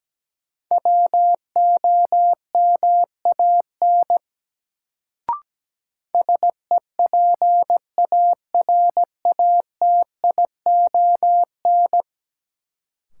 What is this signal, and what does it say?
Morse code, heard as a keyed tone, about 17 wpm, 700 Hz. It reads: WOMAN SEPARATION
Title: Sets of 5 Words - New Word Drill - 17wpm